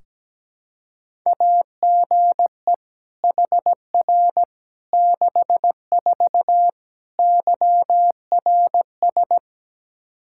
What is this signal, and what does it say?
Morse code, heard as a keyed tone, about 17 wpm, 700 Hz. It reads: AGE HR 64 YRS
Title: Practice QSO - Exchange 3 - More Info - 17wpm